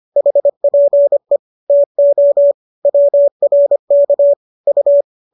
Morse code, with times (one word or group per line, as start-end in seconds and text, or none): 0.2-1.4 HPE
1.7-2.5 TO
2.9-4.3 WRK
4.7-5.0 U